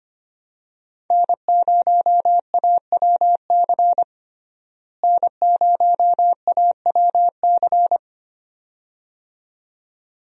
Morse code, heard as a keyed tone, about 25 wpm, 700 Hz. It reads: N0AWC N0AWC
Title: USA Call Signs - 5 Characters - Repeated 2x - 25wpm